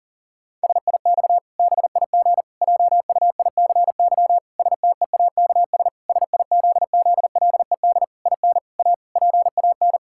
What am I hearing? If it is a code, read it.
Morse code, heard as a keyed tone, about 40 wpm, 700 Hz. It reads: SIX BIG JUICY STEAKS SIZZLED IN A PAN
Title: Pangram Challenge v2 - 40wpm